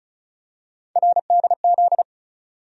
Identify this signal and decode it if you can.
Morse code, heard as a keyed tone, about 34 wpm, 700 Hz. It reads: RDZ
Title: USA Call Sign Postfixes - 3 Letters 35wpm